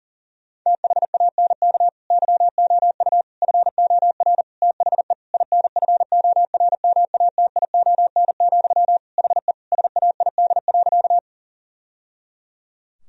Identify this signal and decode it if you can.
Morse code, heard as a keyed tone, about 37 wpm, 700 Hz. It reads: THANK YOU FOR THE INFORMATION, HE SAID.
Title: Sentences from Top 1000 Words - Review Part 3 - 40wpm